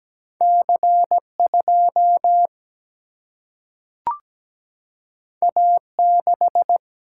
Morse code, heard as a keyed tone, about 17 wpm, 700 Hz. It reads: C 2 A 6
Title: Single-Letter Single-Number 17wpm